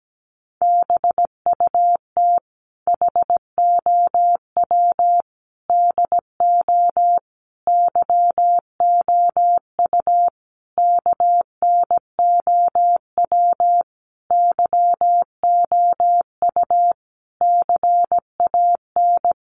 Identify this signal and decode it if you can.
Morse code, heard as a keyed tone, about 17 wpm, 700 Hz. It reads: BUT HOW DO YOU KNOW YOU CAN
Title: Sentences from Top 900 Words 17wpm